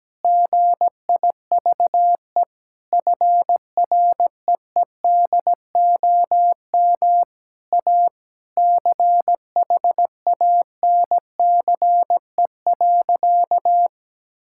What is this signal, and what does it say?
Morse code, heard as a keyed tone, about 17 wpm, 700 Hz. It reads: GIVE FREEDOM A CHANCE.
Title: Sentences from Top 1000 Words - Review Part 5 - 17wpm